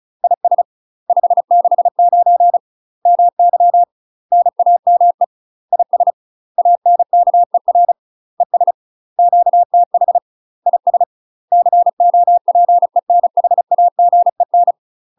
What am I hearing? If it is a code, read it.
Morse code, heard as a keyed tone, about 34 wpm, 700 Hz. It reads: IS 569 MY NAME IS ANKER ES QTH IS COPENHAGEN
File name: UK MASTER QSOs TRAINING / 35WPM / UK MASTER QSOs 0025